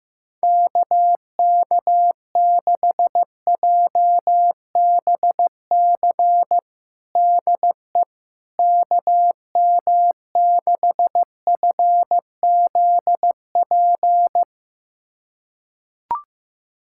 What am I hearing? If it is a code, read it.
Morse code, heard as a keyed tone, about 16 wpm, 700 Hz. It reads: KK6JBC DE KM6FZP
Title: USA Callsign DE Callsign - 6 Characters - 15wpm